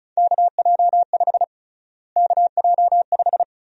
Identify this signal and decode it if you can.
Morse code, heard as a keyed tone, about 36 wpm, 700 Hz. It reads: KJ5 KJ5
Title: USA Call Sign Prefixes - Repeated 2x - 35wpm